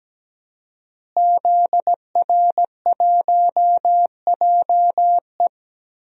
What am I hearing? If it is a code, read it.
Morse code, heard as a keyed tone, about 17 wpm, 700 Hz. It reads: ZR1JE
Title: DXCC-100 Call Signs - 5 Characters - Repeated 3x - 17wpm